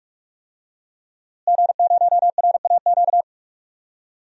Morse code, heard as a keyed tone, about 42 wpm, 700 Hz. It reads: G0RAQ
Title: DXCC-100 Call Signs - 5 Characters - Repeated 2x - 45wpm